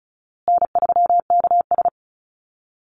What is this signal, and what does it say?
Morse code, heard as a keyed tone, about 34 wpm, 700 Hz. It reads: N3KS
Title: Call Signs Repeated 3x - 35wpm